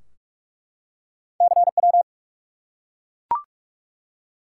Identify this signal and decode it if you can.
Morse code, heard as a keyed tone, about 46 wpm, 700 Hz. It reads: CW
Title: Top 180 QSO Elements 45wpm